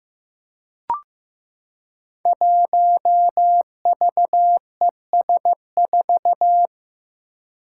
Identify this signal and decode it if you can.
Morse code, heard as a keyed tone, about 15 wpm, 700 Hz. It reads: 1VES4
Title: Five Challenging Letter-Number Combinations 15wpm